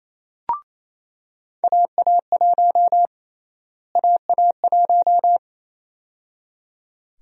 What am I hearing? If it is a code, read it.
Morse code, heard as a keyed tone, about 28 wpm, 700 Hz. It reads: AA1 AA1
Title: USA Call Sign Prefixes - Repeated 2x - 28wpm